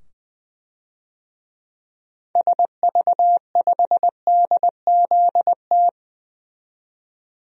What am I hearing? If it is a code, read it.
Morse code, heard as a keyed tone, about 20 wpm, 700 Hz. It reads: SV5DZT
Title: DXCC-100 Call Signs - 6 Characters - 20wpm